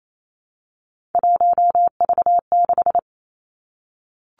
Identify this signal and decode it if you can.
Morse code, heard as a keyed tone, about 28 wpm, 700 Hz. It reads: 1V6
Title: Three Challenging Letter-Number Combinations 28wpm